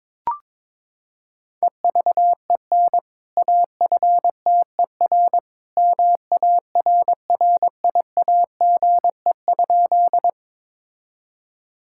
Morse code, heard as a keyed tone, about 21 wpm, 700 Hz. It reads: EVEN AFTER MARRIAGE?
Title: Sentences from Top 1000 Words - Review Part 5 - 22wpm